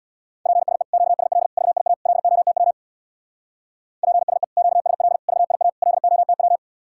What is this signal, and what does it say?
Morse code, heard as a keyed tone, about 60 wpm, 700 Hz. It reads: PSE QSL VIA LOTW PSE QSL VIA LOTW
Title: Speed-Racing Common QSO Phrases Encore - 50wpm